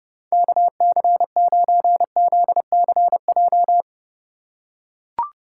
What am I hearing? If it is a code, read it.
Morse code, heard as a keyed tone, about 32 wpm, 700 Hz. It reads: KC9ZCJ
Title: USA Call Signs - 6 Characters - 4x Character Spacing - 30wpm